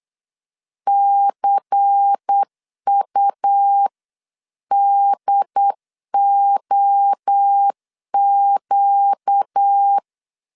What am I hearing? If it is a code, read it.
Morse code, heard as a keyed tone, about 8 wpm, 800 Hz. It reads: CUDOQ
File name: LEZIONI / 7 GF9Z / 7 GF9Z 8wpm 3213